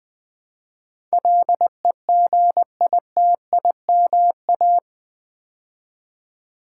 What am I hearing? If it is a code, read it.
Morse code, heard as a keyed tone, about 20 wpm, 700 Hz. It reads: LEGITIMA
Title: Build A Word - 4 Letters At A Time - 10-Letter Words - 20wpm